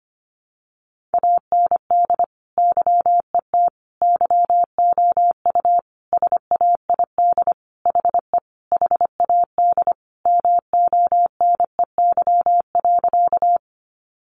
Morse code, heard as a keyed tone, about 25 wpm, 700 Hz. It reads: AND YET YOU SAID HE HAD MONEY.
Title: Sentences from Top 500 Words 25wpm